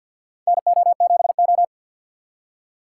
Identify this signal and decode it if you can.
Morse code, heard as a keyed tone, about 52 wpm, 700 Hz. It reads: NO7O
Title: USA Call Signs - 4 Characters - 3x Character Spacing - 50wpm